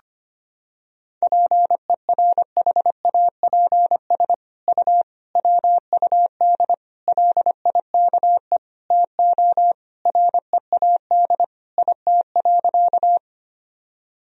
Morse code, heard as a keyed tone, about 25 wpm, 700 Hz. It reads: PERHAPS U WUD LIKE TO READ IT.
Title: Sentences from Top 500 Words and Abbreviations 25wpm